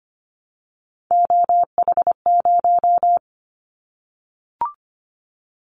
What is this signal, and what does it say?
Morse code, heard as a keyed tone, about 25 wpm, 700 Hz. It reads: OH0
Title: Callsign Prefixes for Top 100 DXCC - 25wpm